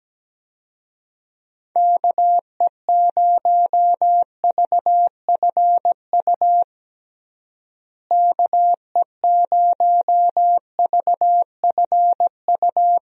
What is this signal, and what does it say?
Morse code, heard as a keyed tone, about 17 wpm, 700 Hz. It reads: KE0VFU KE0VFU
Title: USA Call Signs - 6 Characters - Repeated 2x - 17wpm